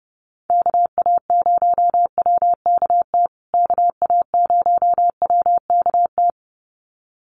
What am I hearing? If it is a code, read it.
Morse code, heard as a keyed tone, about 30 wpm, 700 Hz. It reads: KA0WKT KA0WKT
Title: USA Call Signs Repeated 2x 30wpm